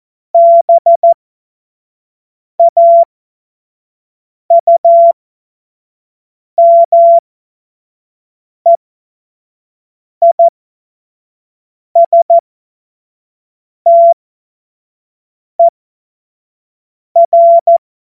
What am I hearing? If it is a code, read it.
Morse code, heard as a keyed tone, about 14 wpm, spaced as for 5 wpm, 680 Hz. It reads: BAUMEISTER
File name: Morsekurs / Witze / Witz 07.November 2023 25 F70